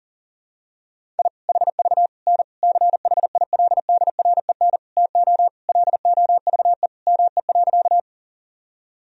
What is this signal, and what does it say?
Morse code, heard as a keyed tone, about 37 wpm, 700 Hz. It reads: I HV N CHILDREN TO LOVE ME.
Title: Sentences from Top 400 Words and Abbreviations 40wpm